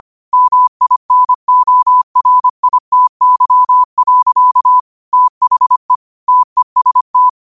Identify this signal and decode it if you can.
Morse code, heard as a keyed tone, about 25 wpm, 1000 Hz. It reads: MINORITY. THE TEST
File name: qotdfiles / 25wpm